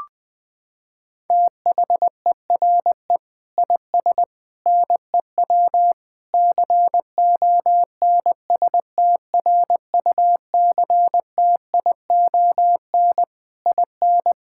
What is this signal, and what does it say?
Morse code, heard as a keyed tone, about 20 wpm, 700 Hz. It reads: THERE IS NEW CONSTRUCTION IN
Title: Sentences from Top 1600 Words 20wpm